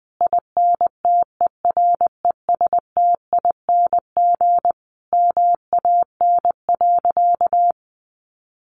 Morse code, heard as a keyed tone, about 20 wpm, 700 Hz. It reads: INTERESTING MAN.
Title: Sentences from Top 1000 Words 20wpm